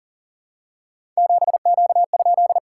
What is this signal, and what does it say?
Morse code, heard as a keyed tone, about 40 wpm, 690 Hz. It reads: 7Q?
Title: SOTA QSO Practice 40wpm